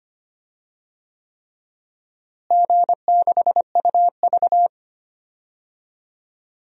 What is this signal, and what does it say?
Morse code, heard as a keyed tone, about 25 wpm, 700 Hz. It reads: G6UV
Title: DXCC-100 Call Signs - 4 Characters - Repeated 3x - 25wpm